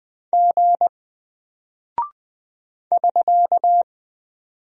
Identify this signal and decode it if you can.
Morse code, heard as a keyed tone, about 20 wpm, 700 Hz. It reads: G <SK>
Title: Single Character (Letter:Number:Punctuation:Prosign) Alt 20wpm